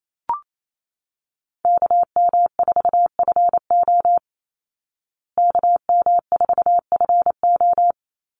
Morse code, heard as a keyed tone, about 28 wpm, 700 Hz. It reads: KM4FO KM4FO
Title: Call Signs Repeated 3x - 28wpm